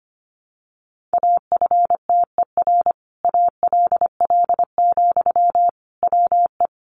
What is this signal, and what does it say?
Morse code, heard as a keyed tone, about 25 wpm, 700 Hz. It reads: AFTER ALL, WE
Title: Sentences from Top 1300 Words 25wpm